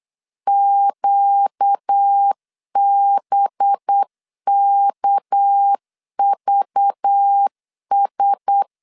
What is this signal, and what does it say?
Morse code, heard as a keyed tone, about 8 wpm, 800 Hz. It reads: QBKVS
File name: CW / DET / 1 9 1500C S B 8wpm 5624